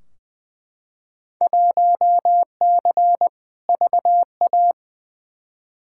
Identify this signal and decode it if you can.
Morse code, heard as a keyed tone, about 20 wpm, 700 Hz. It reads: 1C VA
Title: Field Day Exchanges Encore - 20wpm